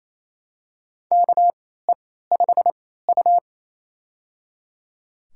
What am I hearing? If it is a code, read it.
Morse code, heard as a keyed tone, about 28 wpm, spaced as for 16 wpm, 700 Hz. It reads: KE5U
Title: USA Call Signs - 4 Characters - 3x Character Spacing - 28wpm